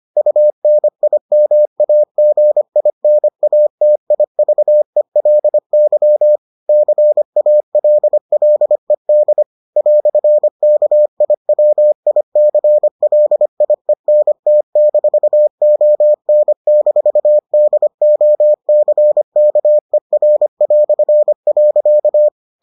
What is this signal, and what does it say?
Morse code, heard as a keyed tone, about 25 wpm, 600 Hz. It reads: UNIMAGINATIVELY CALLED "KIWICLIENT-ON-DOCKER".